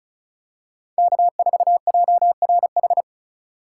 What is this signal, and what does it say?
Morse code, heard as a keyed tone, about 36 wpm, 700 Hz. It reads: K4JRH
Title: USA Call Signs - 5 Characters - 2x Character Spacing - 35wpm